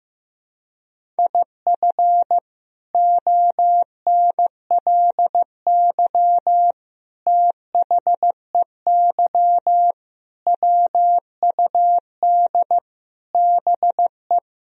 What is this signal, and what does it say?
Morse code, heard as a keyed tone, about 14 wpm, 700 Hz. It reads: IF ONLY THEY WUD BE
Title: Sentences from Top 500 Words and Abbreviations 15wpm